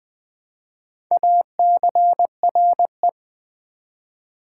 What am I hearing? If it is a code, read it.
Morse code, heard as a keyed tone, about 20 wpm, 700 Hz. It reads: ACRE
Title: Build A Word - 1 Letter At A Time - 4-Letter Words - 20wpm